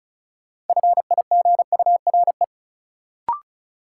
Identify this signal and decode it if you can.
Morse code, heard as a keyed tone, about 36 wpm, 700 Hz. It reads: FIGURE
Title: Sets of 4 Words - New Word Drill - 35wpm